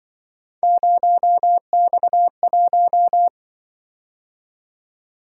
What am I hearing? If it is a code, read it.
Morse code, heard as a keyed tone, about 25 wpm, 700 Hz. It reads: 0X1
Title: Speed-Racing - Three Letter-Number Combinations - 20wpm